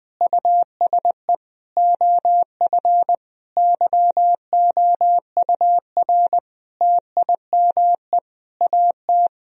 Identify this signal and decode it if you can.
Morse code, heard as a keyed tone, about 21 wpm, 700 Hz. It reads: USE OF YOUR TIME AT
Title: Sentences from Top 1000 Words - Review Part 1 - 20wpm